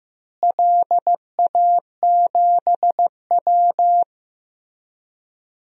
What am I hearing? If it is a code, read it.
Morse code, heard as a keyed tone, about 15 wpm, 700 Hz. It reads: LA7W
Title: DXCC-100 Call Signs - 4 Characters - Repeated 3x - 15wpm